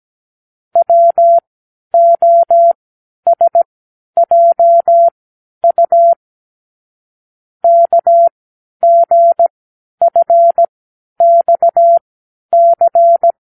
Morse code, heard as a keyed tone, about 17 wpm, spaced as for 10 wpm, 680 Hz. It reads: WOSJU KGFXC
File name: k7qo / Lesson 36 (A-X)